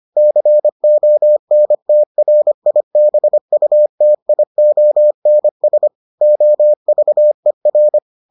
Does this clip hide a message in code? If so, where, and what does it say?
Yes: Morse code from 0.2 to 8.0 s, heard as a keyed tone, about 25 wpm, 600 Hz. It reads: CONTRIBUTIONS OVER